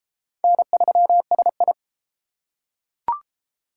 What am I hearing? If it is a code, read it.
Morse code, heard as a keyed tone, about 29 wpm, 700 Hz. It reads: N3SI